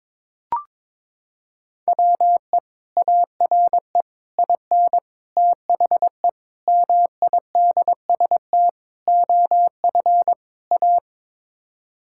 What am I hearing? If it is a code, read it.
Morse code, heard as a keyed tone, about 22 wpm, 700 Hz. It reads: WE ARE IN THE MIDST OF A
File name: practice / 22wpm / Sets of 7 Words - 22wpm